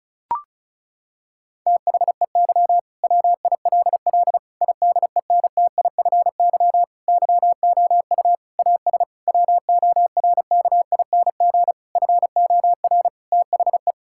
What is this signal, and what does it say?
Morse code, heard as a keyed tone, about 34 wpm, 700 Hz. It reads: THEY WILL IDENTIFY YOU AS WORKING FOR THE
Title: Sentences from Top 1000 Words - Review Part 6 - 35wpm